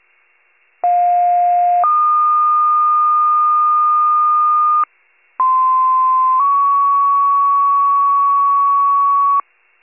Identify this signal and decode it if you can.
Two-tone page, heard as a one-second tone, then a three-second tone, about 710 then 1220 Hz, about 1040 then 1120 Hz. It is heard from 0.8 to 9.4 s.
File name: pager / 46 RESCUE / wcalert